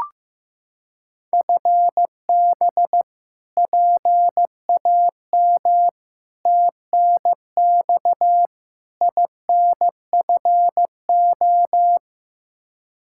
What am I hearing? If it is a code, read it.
Morse code, heard as a keyed tone, about 15 wpm, 700 Hz. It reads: FB PAM TNX INFO
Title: Practice QSO - Exchange 3 - More Info - 15wpm